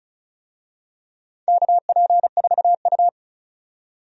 Morse code, heard as a keyed tone, about 36 wpm, 700 Hz. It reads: KP4U
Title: DXCC-100 Call Signs - 4 Characters - 35wpm